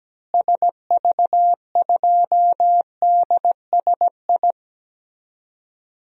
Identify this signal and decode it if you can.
Morse code, heard as a keyed tone, about 17 wpm, 700 Hz. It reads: SV2DSI
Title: DXCC-100 Call Signs - 6 Characters - Repeated 3x - 17wpm